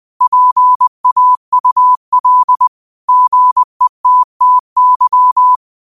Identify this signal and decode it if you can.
Morse code, heard as a keyed tone, about 20 wpm, 1000 Hz. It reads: PAUL GETTY
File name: qotdfiles / 20wpm